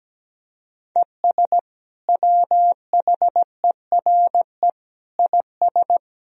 Morse code, heard as a keyed tone, about 17 wpm, 700 Hz. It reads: ES WHERE IS
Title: Sentences from Top 200 Words and Abbreviations 17wpm